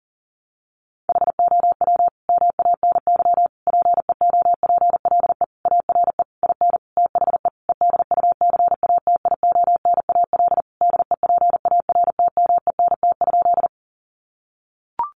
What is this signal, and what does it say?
Morse code, heard as a keyed tone, about 40 wpm, 700 Hz. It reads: HOW MANY PEOPLE ARE IN THE EDUCATIONAL DEPARTMENT?
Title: Sentences from Top 1200 Words 40wpm